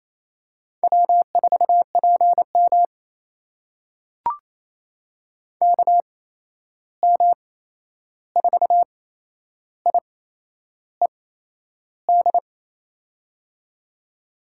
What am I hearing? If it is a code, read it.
Morse code, heard as a keyed tone, about 28 wpm, spaced as for 11 wpm, 700 Hz. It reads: W4PM KM4IED
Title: USA Call Signs - 8x Character Spacing - 28wpm